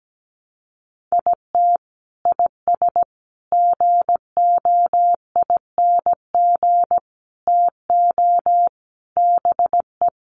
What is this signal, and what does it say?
Morse code, heard as a keyed tone, about 17 wpm, 700 Hz. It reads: IT IS GOING TO BE
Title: Sets of 5 Words 17wpm